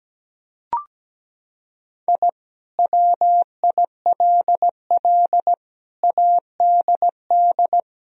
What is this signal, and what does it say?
Morse code, heard as a keyed tone, about 17 wpm, 700 Hz. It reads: I WILL ADD
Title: Sentences from Top 1000 Words - Review Part 1 - 17wpm